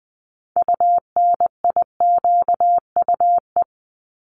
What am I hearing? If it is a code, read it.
Morse code, heard as a keyed tone, about 20 wpm, 700 Hz. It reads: UNIQUE
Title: Multi-letter Phonemes 20wpm